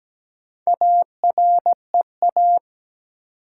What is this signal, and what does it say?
Morse code, heard as a keyed tone, about 17 wpm, 700 Hz. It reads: AREA
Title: Sentences from Top 1000 Words - Review Part 2 - New Word Drill - 17wpm